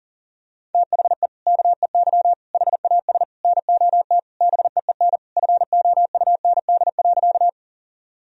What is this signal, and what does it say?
Morse code, heard as a keyed tone, about 40 wpm, 700 Hz. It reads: THE KEY HAS NOT BEEN FOUND.
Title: Sentences from Top 1000 Words - Review Part 4 - 40wpm